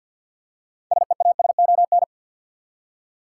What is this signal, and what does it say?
Morse code, heard as a keyed tone, about 53 wpm, 700 Hz. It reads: SEASON
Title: Six Letter Words - 50wpm